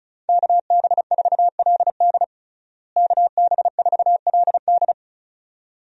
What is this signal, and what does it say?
Morse code, heard as a keyed tone, about 34 wpm, 700 Hz. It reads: KB4LD KB4LD
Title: USA Call Signs - 5 Characters - Repeated 2x - 35wpm